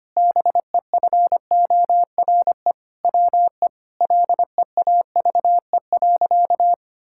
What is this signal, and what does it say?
Morse code, heard as a keyed tone, about 25 wpm, 700 Hz. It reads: BEFORE WE LEAVE.
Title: Sentences from Top 1000 Words - Review Part 5 - 25wpm